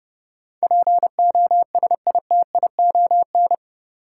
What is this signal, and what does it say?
Morse code, heard as a keyed tone, about 30 wpm, 700 Hz. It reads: POSITION
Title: Sentences from Top 900 Words - New Word Drill - 30wpm